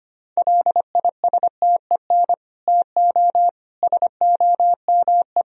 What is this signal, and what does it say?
Morse code, heard as a keyed tone, about 25 wpm, 700 Hz. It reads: LISTEN TO SOME
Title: Sentences from Top 1000 Words - Review Part 1 - 25wpm